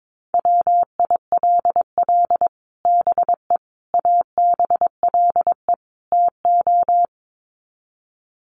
Morse code, heard as a keyed tone, about 22 wpm, 700 Hz. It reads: WILL BE ABLE TO
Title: Sets of 4 Words 22wpm